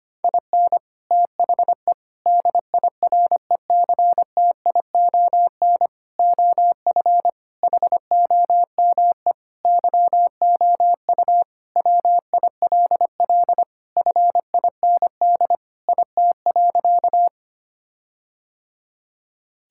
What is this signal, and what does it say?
Morse code, heard as a keyed tone, about 25 wpm, 700 Hz. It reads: IN THE DIRECTION OF HOME YOU WILL FIND IT.
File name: practice / 25wpm / Sentences with 4-Word Phrases - 25wpm